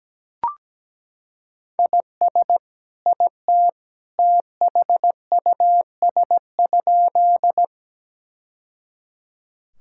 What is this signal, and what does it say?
Morse code, heard as a keyed tone, about 17 wpm, 700 Hz. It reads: IS IT THUS?
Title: Sentences from Top 1000 Words - Review Part 5 - 17wpm